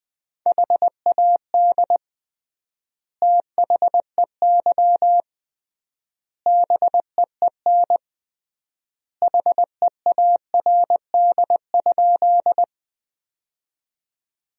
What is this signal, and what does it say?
Morse code, heard as a keyed tone, about 20 wpm, 700 Hz. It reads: HAD THEY BEEN HEARD?
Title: 3x Word-Spacing - Sentences from Top 400 Words - 20wpm